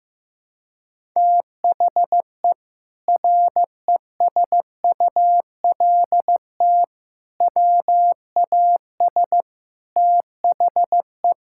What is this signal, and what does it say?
Morse code, heard as a keyed tone, about 15 wpm, 700 Hz. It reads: THE RESULT WAS THE
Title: Sentences from Top 1000 Words - Review Part 5 - 15wpm